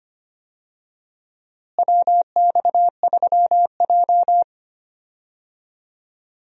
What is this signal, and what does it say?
Morse code, heard as a keyed tone, about 25 wpm, 700 Hz. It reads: WX3J
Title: USA Call Signs - 4 Characters - Repeated 2x - 25wpm